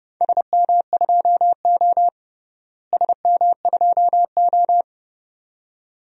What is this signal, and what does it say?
Morse code, heard as a keyed tone, about 32 wpm, 700 Hz. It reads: SM2O SM2O
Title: DXCC-100 Call Signs - 4 Characters - Repeated 2x - 30wpm